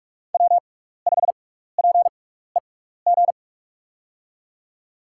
Morse code, heard as a keyed tone, about 44 wpm, spaced as for 15 wpm, 700 Hz. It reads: W5PEG
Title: USA Call Signs - 6x Character Spacing - 45wpm